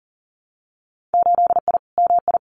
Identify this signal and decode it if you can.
Morse code, heard as a keyed tone, about 39 wpm, 700 Hz. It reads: 8I MI
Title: Winter Field Day Exchanges Encore 40wpm